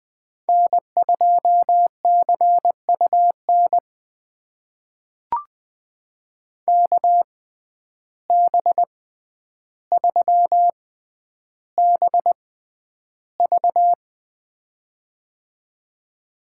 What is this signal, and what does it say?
Morse code, heard as a keyed tone, about 20 wpm, spaced as for 10 wpm, 700 Hz. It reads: N2CUN KB3BV
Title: USA Call Signs - 5 Characters - 6x Character Spacing - 20wpm